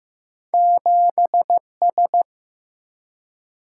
Morse code, heard as a keyed tone, about 15 wpm, 700 Hz. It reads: 7 S
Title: Single-Letter Single-Number 15wpm